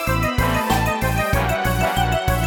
There is no code.